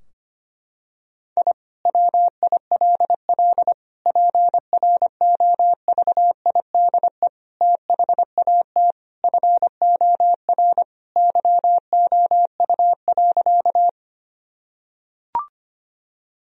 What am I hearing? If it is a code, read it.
Morse code, heard as a keyed tone, about 25 wpm, 700 Hz. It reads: I WILL PROVIDE THAT FOR YOU.
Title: Sentences from Top 1000 Words - Review Part 3 - 25wpm